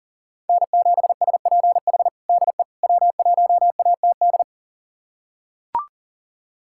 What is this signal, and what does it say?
Morse code, heard as a keyed tone, about 40 wpm, 700 Hz. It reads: N7SPH DE W1ATD
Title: USA Callsign DE Callsign - 5 Characters - 40wpm